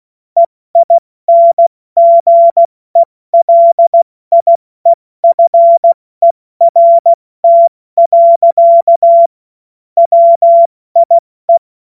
Morse code, heard as a keyed tone, about 16 wpm, 680 Hz. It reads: EINGELIEFERT. WIE
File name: Witz-09-September-2025-70-F80